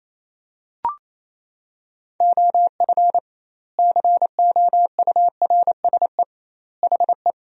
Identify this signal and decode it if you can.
Morse code, heard as a keyed tone, about 28 wpm, 700 Hz. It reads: OF COURSE HE
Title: 2x Word-Spacing - Sentences from Top 500 Words - 28wpm